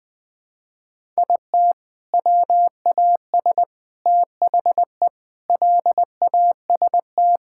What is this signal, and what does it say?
Morse code, heard as a keyed tone, about 20 wpm, 700 Hz. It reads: IT WAS THE LAST
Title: Sentences with 3-Word Phrases - Spoken Beginning - 20wpm